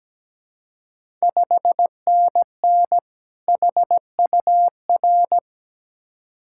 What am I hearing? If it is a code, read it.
Morse code, heard as a keyed tone, about 17 wpm, 690 Hz. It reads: 5NN HUR
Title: Ontario QSO Party Exchange 17wpm